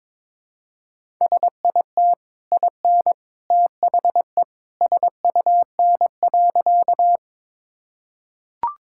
Morse code, heard as a keyed tone, about 22 wpm, 690 Hz. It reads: SIT IN THE SUN.
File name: practice / 22wpm / Easy Sentences with Hints - 22wpm